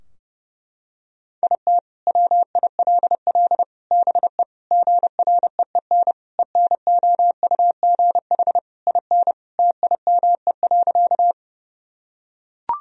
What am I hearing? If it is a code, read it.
Morse code, heard as a keyed tone, about 32 wpm, 700 Hz. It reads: IT WILL BE GREEN ENOUGH IN TIME.
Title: Sentences from Top 1000 Words - Review Part 2 - 30wpm